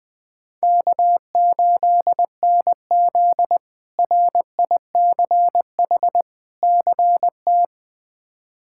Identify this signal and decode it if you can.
Morse code, heard as a keyed tone, about 20 wpm, 700 Hz. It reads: K8NZ RICH CT
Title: K1USN SST QSO Practice - 20wpm